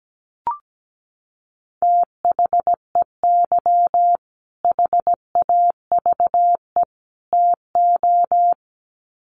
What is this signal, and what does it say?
Morse code, heard as a keyed tone, about 17 wpm, 700 Hz. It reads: THEY HAVE TO
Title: Sets of 3 Words 17wpm